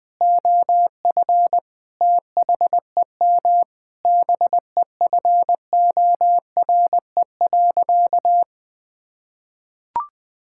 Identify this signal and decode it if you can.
Morse code, heard as a keyed tone, about 20 wpm, 700 Hz. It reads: OF THEM BEFORE.
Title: Sentences with 9-Word Phrases - 20wpm